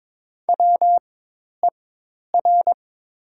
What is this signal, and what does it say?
Morse code, heard as a keyed tone, about 22 wpm, spaced as for 10 wpm, 700 Hz. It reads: WER
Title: 4x Character Spacing - Top 200 Words - 22wpm